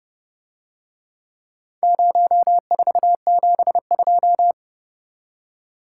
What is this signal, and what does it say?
Morse code, heard as a keyed tone, about 29 wpm, 700 Hz. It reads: 0472